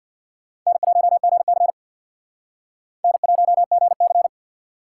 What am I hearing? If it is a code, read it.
Morse code, heard as a keyed tone, about 62 wpm, 700 Hz. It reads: N1GC N1GC
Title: Speed Racing - USA 1x2 Call Signs 50wpm